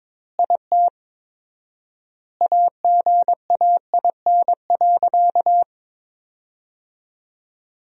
Morse code, heard as a keyed tone, about 22 wpm, 700 Hz. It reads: IT AGAIN.
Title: 4x Word-Spacing - Sentences from Top 300 Words - 22wpm